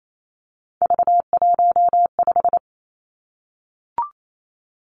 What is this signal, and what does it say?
Morse code, heard as a keyed tone, about 28 wpm, 700 Hz. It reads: V15
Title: Three Challenging Letter-Number Combinations 28wpm